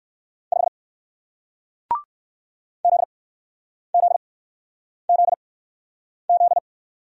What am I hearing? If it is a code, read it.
Morse code, heard as a keyed tone, about 61 wpm, spaced as for 9 wpm, 700 Hz. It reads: 5 ZZZZ